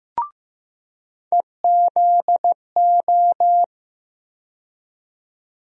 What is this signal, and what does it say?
Morse code, heard as a keyed tone, about 15 wpm, 700 Hz. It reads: EZO
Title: USA Call Sign Postfixes - 3 Letters 15wpm